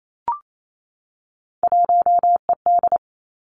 Morse code, heard as a keyed tone, about 28 wpm, 700 Hz. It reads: 1ED
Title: Three Challenging Letter-Number Combinations 28wpm